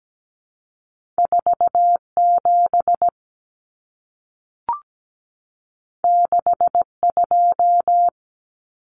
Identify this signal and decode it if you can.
Morse code, heard as a keyed tone, about 17 wpm, 700 Hz. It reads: 47 62
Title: Numbers - 2-Digit 17wpm